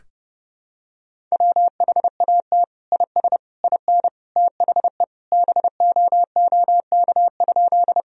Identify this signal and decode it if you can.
Morse code, heard as a keyed tone, about 30 wpm, 700 Hz. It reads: WHAT IS IN THE BOOK?
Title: Sentences from Top 1000 Words - Review Part 1 - 30wpm